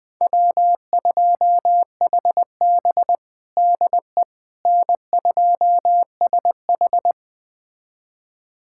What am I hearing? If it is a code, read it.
Morse code, heard as a keyed tone, about 20 wpm, 700 Hz. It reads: W2HB DE N2SH
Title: USA Callsign DE Callsign - 4 Characters - 20wpm